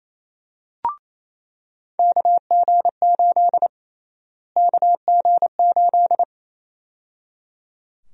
Morse code, heard as a keyed tone, about 28 wpm, 700 Hz. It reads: KG8 KG8
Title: USA Call Sign Prefixes - Repeated 2x - 28wpm